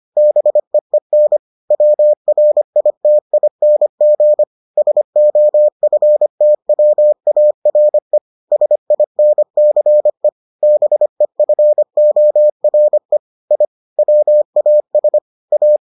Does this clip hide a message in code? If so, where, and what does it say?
Yes: Morse code from 0.2 to 15.8 s, heard as a keyed tone, about 25 wpm, 600 Hz. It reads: BEEN WRITING SOFTWARE SINCE BEFORE I WAS A